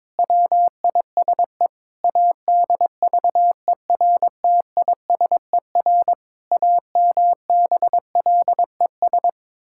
Morse code, heard as a keyed tone, about 22 wpm, 700 Hz. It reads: WISE ADVERTISER AMBLES
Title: Pangram Challenge v2 - 22wpm